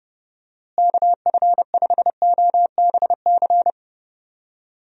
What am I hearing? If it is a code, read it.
Morse code, heard as a keyed tone, about 32 wpm, 700 Hz. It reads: KF5OBC
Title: USA Call Signs - 6 Characters - Repeated 2x - 30wpm